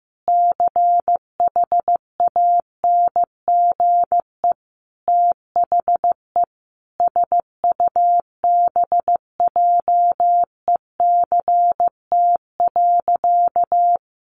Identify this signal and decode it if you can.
Morse code, heard as a keyed tone, about 15 wpm, 700 Hz. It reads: CHANGE THE SUBJECT.
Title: Sentences from Top 800 Words 15wpm